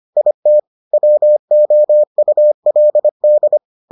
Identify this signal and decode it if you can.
Morse code, heard as a keyed tone, about 25 wpm, 600 Hz. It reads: IT WOULD